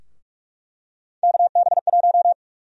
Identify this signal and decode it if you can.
Morse code, heard as a keyed tone, about 44 wpm, 700 Hz. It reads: KB1
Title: USA Call Sign Prefixes - 45wpm